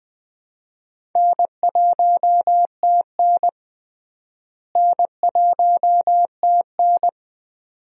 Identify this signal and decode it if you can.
Morse code, heard as a keyed tone, about 20 wpm, 700 Hz. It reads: N1TN N1TN
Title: USA Call Signs - 4 Characters - Repeated 2x - 20wpm